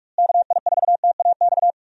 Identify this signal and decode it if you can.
Morse code, heard as a keyed tone, about 44 wpm, 700 Hz. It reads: KI4TAX
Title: USA Call Signs - 6 Characters - Repeated 3x - 45wpm